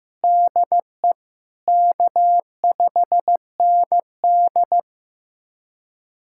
Morse code, heard as a keyed tone, about 15 wpm, 700 Hz. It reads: DE K5ND
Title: USA Callsign DE Callsign - 4 Characters - 15wpm